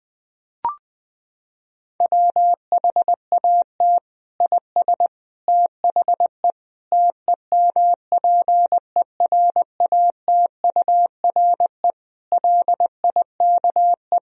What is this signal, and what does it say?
Morse code, heard as a keyed tone, about 20 wpm, 700 Hz. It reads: WHAT IS THE TEMPERATURE LIKE
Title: Sentences from Top 1000 Words - Review Part 4 - 20wpm